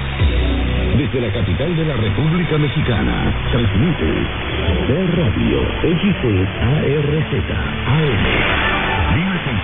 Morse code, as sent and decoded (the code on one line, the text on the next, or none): none